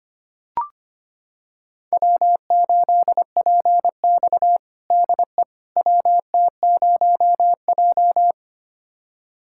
Morse code, heard as a keyed tone, about 25 wpm, 700 Hz. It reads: W8PX DE WT0J
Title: USA Callsign DE Callsign - 4 Characters - 25wpm